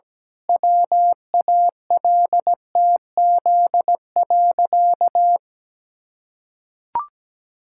Morse code, heard as a keyed tone, about 17 wpm, 700 Hz. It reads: WALTZ.
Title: Pangram Challenge v2 - 17wpm